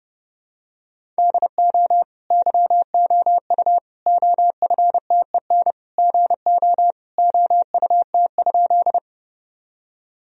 Morse code, heard as a keyed tone, about 30 wpm, 700 Hz. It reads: DO YOU OFTEN GO OUT?
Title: Sentences from Top 1000 Words - Review Part 1 - 30wpm